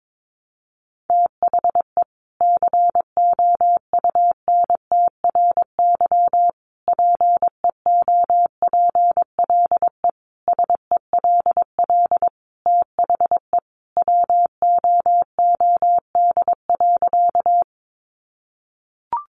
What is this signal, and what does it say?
Morse code, heard as a keyed tone, about 22 wpm, 700 Hz. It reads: THE COUNTRY PEOPLE SELL THE WOOD.
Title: Sentences from Top 1100 Words 22wpm